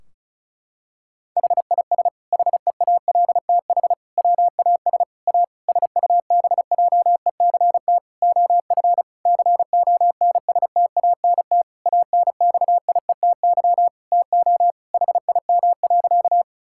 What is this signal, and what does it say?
Morse code, heard as a keyed tone, about 34 wpm, 700 Hz. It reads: HIS HEALTH WAS A SUBJECT OF CONSTANT ANXIETY TO HIM.
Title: Sentences from Top 1600 Words 35wpm